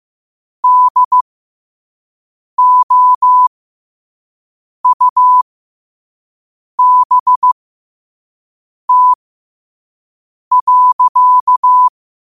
Morse code, heard as a keyed tone, about 15 wpm, spaced as for 5 wpm, 1000 Hz. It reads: DOUBT.
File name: qotdfiles / 5wpm